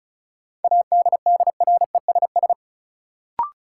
Morse code, heard as a keyed tone, about 34 wpm, 700 Hz. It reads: ADDRESS
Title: Sets of 5 Words Encore - New Word Drill - 35wpm